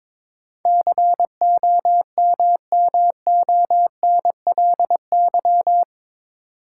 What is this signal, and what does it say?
Morse code, heard as a keyed tone, about 21 wpm, 700 Hz. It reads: COMMONLY
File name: practice / 22wpm / Eight Letter Words - 22wpm